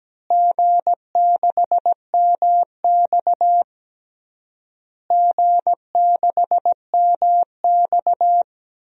Morse code, heard as a keyed tone, about 17 wpm, 700 Hz. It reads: G6MX G6MX
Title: DXCC-100 Call Signs - 4 Characters - Repeated 3x - 17wpm